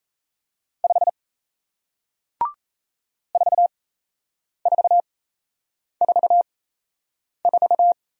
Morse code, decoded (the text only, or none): 5 4444